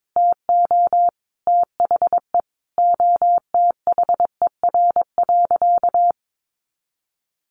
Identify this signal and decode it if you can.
Morse code, heard as a keyed tone, about 22 wpm, 700 Hz. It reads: TO THE OTHER.
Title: Sentences from Top 1300 Words 22wpm